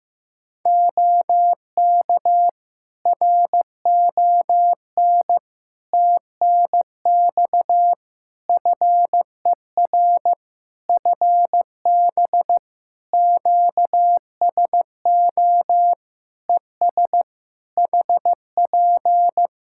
Morse code, heard as a keyed tone, about 15 wpm, 700 Hz. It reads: OK RON TNX FER FB QSO ES HP
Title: Practice QSO - Exchange 4 - Ending - 15wpm